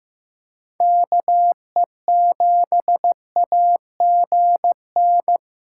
Morse code, heard as a keyed tone, about 18 wpm, 700 Hz. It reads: KE7AGN
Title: Speed Racing - USA 2x3 Call Signs 15wpm